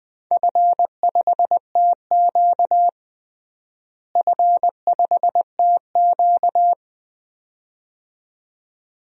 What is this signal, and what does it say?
Morse code, heard as a keyed tone, about 20 wpm, 700 Hz. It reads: F5TQ F5TQ
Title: DXCC-100 Call Signs - 4 Characters - Repeated 2x - 20wpm